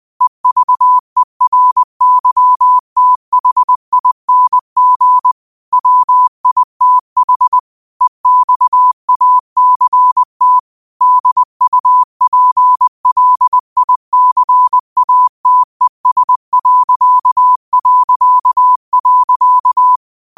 Morse code, heard as a keyed tone, about 20 wpm, 1000 Hz. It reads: EVERYTHING WITH EXACT DUPLICATES...
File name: 2025-06-25-20wpm